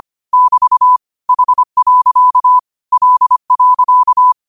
Morse code, heard as a keyed tone, about 25 wpm, 1000 Hz. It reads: <BT> H. L.
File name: qotdfiles / 25wpm